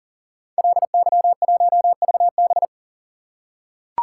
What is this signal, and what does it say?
Morse code, heard as a keyed tone, about 38 wpm, 700 Hz. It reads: LY1VB
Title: Code Groups - 5 Letter:Numbers - 40wpm